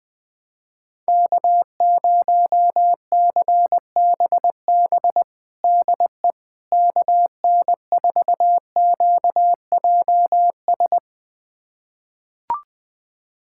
Morse code, heard as a keyed tone, about 20 wpm, 700 Hz. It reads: K0CBB DE KN4QJS
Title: USA Callsign DE Callsign 20wpm